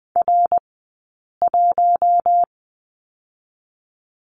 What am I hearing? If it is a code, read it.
Morse code, heard as a keyed tone, about 20 wpm, 700 Hz. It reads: R 1
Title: 080 Lesson 14d - Number 2 - Two Characters 20wpm